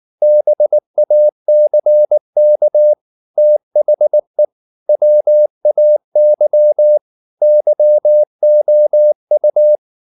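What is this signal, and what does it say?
Morse code, heard as a keyed tone, about 19 wpm, 600 Hz. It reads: BACK THE WAY YOU
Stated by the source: Morse code at 19 wpm from CW med Gnister, reading Ken Follett's novel Never in English.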